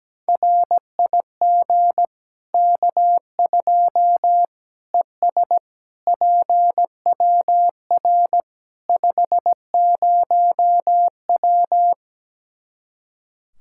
Morse code, heard as a keyed tone, about 17 wpm, 700 Hz. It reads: RIG K2 ES PWR 50W
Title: Practice QSO - Exchange 2 - Info - 17wpm